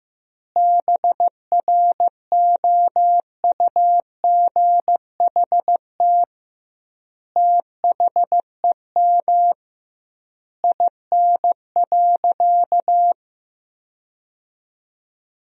Morse code, heard as a keyed tone, about 15 wpm, 700 Hz. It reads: BROUGHT THEM IN.
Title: 2x Word-Spacing - Sentences from Top 500 Words - 15wpm